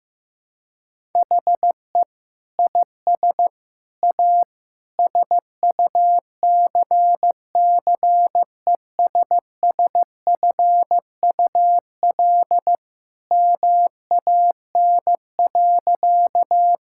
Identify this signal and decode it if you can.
Morse code, heard as a keyed tone, about 15 wpm, 700 Hz. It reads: HE IS A SUCCESSFUL MAN.
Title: Sentences from Top 1000 Words - Review Part 6 - 15wpm